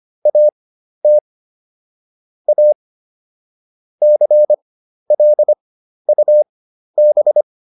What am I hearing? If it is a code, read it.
Morse code, heard as a keyed tone, about 25 wpm, spaced as for 12 wpm, 600 Hz. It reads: AT A CLUB